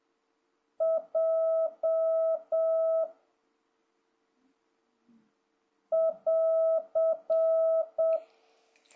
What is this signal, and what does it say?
Morse code, heard as a keyed tone, about 7 wpm, 650 Hz. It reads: J <AR>
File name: jh8cbh / 2026cw